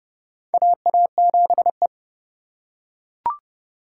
Morse code, heard as a keyed tone, about 32 wpm, 700 Hz. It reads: AA7E
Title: USA Call Signs - 4 Characters - 4x Character Spacing - 30wpm